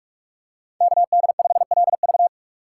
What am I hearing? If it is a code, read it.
Morse code, heard as a keyed tone, about 43 wpm, 700 Hz. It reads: KD5LV
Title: USA Call Signs - 5 Characters - 2x Character Spacing - 45wpm